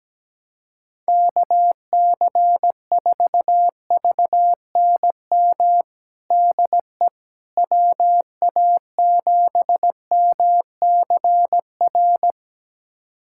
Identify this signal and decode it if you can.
Morse code, heard as a keyed tone, about 17 wpm, 700 Hz. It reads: KC4VNM DE WA7MCR
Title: USA Callsign DE Callsign - 6 Characters - 17wpm